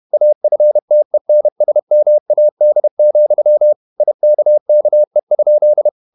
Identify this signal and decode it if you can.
Morse code, heard as a keyed tone, about 31 wpm, 600 Hz. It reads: AFTENSMAD, IKKE?
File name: Ferie QRQ 31wpm